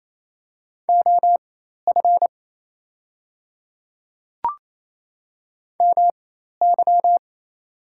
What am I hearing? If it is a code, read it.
Morse code, heard as a keyed tone, about 28 wpm, spaced as for 13 wpm, 700 Hz. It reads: OF MY
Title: 4x Character Spacing - Top 300 Words - 28wpm